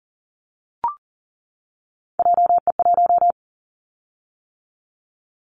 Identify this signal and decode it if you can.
Morse code, heard as a keyed tone, about 38 wpm, 700 Hz. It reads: JE1
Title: Callsign Prefixes for Top 25 DXCC - 40wpm